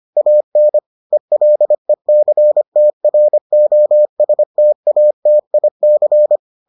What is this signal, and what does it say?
Morse code, heard as a keyed tone, about 25 wpm, 600 Hz. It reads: AN ELECTROSTATIC